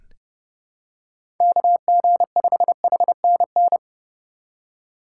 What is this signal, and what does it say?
Morse code, heard as a keyed tone, about 30 wpm, 700 Hz. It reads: KG5HNN
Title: USA Call Signs - 3x Character Spacing - 30wpm